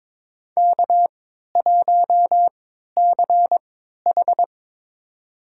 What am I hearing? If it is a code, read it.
Morse code, heard as a keyed tone, about 22 wpm, spaced as for 12 wpm, 700 Hz. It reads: K1CH
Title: USA Call Signs - 4 Characters - 3x Character Spacing - 22wpm